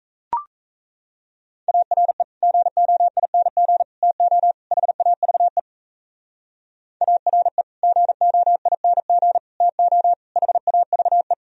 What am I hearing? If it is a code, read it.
Morse code, heard as a keyed tone, about 37 wpm, 700 Hz. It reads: ARE GOING TO HAVE ARE GOING TO HAVE